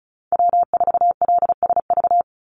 Morse code, heard as a keyed tone, about 34 wpm, 700 Hz. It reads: W4LSV
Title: Call Signs Repeated 2x - 35wpm